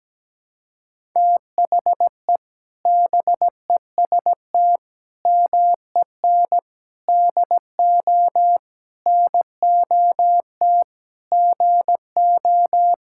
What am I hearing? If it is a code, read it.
Morse code, heard as a keyed tone, about 17 wpm, 700 Hz. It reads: THE BEST MEN DO NOT GO
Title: Sentences from Top 1000 Words - Review Part 6 - 17wpm